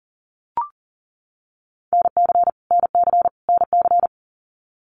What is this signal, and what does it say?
Morse code, heard as a keyed tone, about 40 wpm, 700 Hz. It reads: NC NC NC
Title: US States Repeated 3x 40wpm